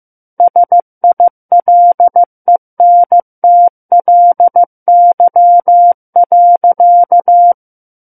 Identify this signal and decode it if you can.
Morse code, heard as a keyed tone, about 15 wpm, 700 Hz. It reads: SILENTLY.
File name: Book One / 06-Chapter 6 - 15 WPM